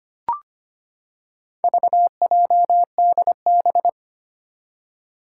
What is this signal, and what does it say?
Morse code, heard as a keyed tone, about 25 wpm, 700 Hz. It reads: VJDB
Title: Four Challenging Letter-Number Combinations 25wpm